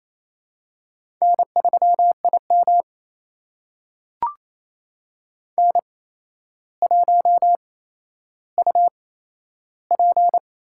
Morse code, heard as a keyed tone, about 28 wpm, spaced as for 12 wpm, 700 Hz. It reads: N3IM N1UP
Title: USA Call Signs - 4 Characters - 8x Character Spacing - 28wpm